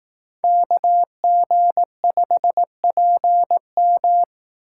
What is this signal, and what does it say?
Morse code, heard as a keyed tone, about 18 wpm, 700 Hz. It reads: KG5PM
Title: Speed Racing - USA 2x2 Call Signs 15wpm